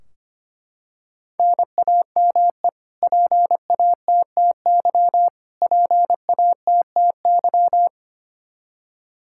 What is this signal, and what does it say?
Morse code, heard as a keyed tone, about 25 wpm, 700 Hz. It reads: NAME PATTY PATTY
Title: Practice QSO - Exchange 1 - Beginning - 25wpm